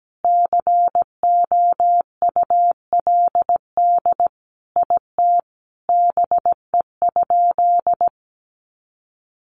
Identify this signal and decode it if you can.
Morse code, heard as a keyed tone, about 17 wpm, 700 Hz. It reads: COULD IT BE?
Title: Sentences from Top 1200 Words 17wpm